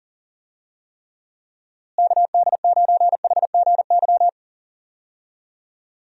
Morse code, heard as a keyed tone, about 40 wpm, 700 Hz. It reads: KD9HGY
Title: USA Call Signs - 6 Characters - Repeated 2x - 40wpm